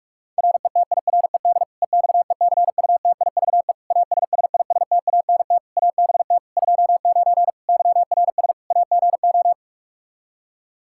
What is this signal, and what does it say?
Morse code, heard as a keyed tone, about 46 wpm, 700 Hz. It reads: RETIRED EXECUTIVE ASSISTANT ABT 29 YRS AGO
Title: Practice QSO - Exchange 3 - More Info - 45wpm